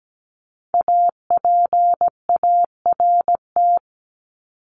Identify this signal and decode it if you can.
Morse code, heard as a keyed tone, about 17 wpm, 700 Hz. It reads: APART
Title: Top 1501-1750 Words 17wpm